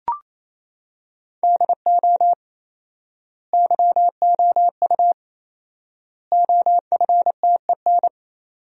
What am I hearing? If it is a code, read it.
Morse code, heard as a keyed tone, about 28 wpm, 700 Hz. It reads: DO YOU OFTEN
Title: 4x Word-Spacing - Sentences from Top 300 Words - 28wpm